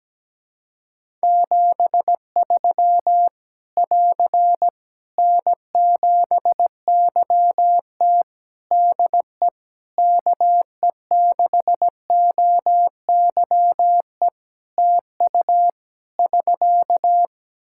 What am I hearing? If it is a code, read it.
Morse code, heard as a keyed tone, about 17 wpm, 700 Hz. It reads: 73 <AR> N7YT DE KE6OYE TU <SK>
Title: Practice QSO - Exchange 4 - Ending - 17wpm